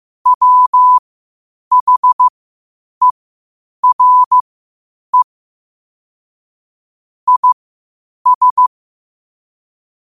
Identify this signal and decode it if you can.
Morse code, heard as a keyed tone, about 15 wpm, spaced as for 8 wpm, 1000 Hz. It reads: WHERE IS
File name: qotdfiles / 8wpm